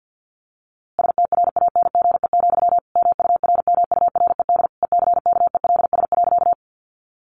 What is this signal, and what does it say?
Morse code, heard as a keyed tone, about 50 wpm, 700 Hz. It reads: STRANGE, MURMURED ECKELS.
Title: A Sound Of Thunder 50wpm